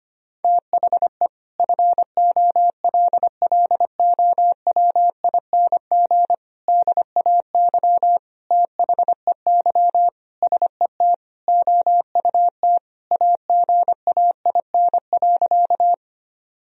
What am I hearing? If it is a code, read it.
Morse code, heard as a keyed tone, about 25 wpm, 700 Hz. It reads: THE FOLLOWING DAY THEY SET OUT AGAIN.
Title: Sentences from Top 1000 Words - Review Part 4 - 25wpm